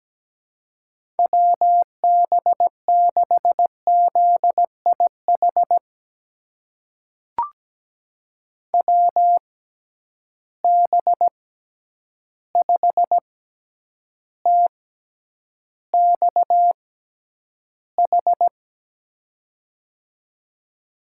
Morse code, heard as a keyed tone, about 17 wpm, spaced as for 9 wpm, 700 Hz. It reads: WB6ZIH WB5TXH
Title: USA Call Signs - 6 Characters - 6x Character Spacing - 17wpm